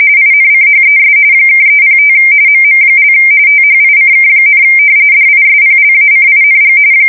RTTY, about 45.45 baud. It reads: INCHES (305 MM) THICK, DISPLACING
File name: fsk170 / battleships